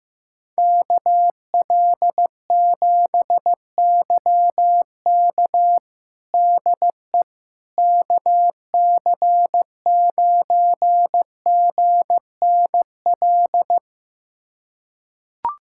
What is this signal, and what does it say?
Morse code, heard as a keyed tone, about 15 wpm, 700 Hz. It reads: KL7YK DE KC9GNL
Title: USA Callsign DE Callsign 15wpm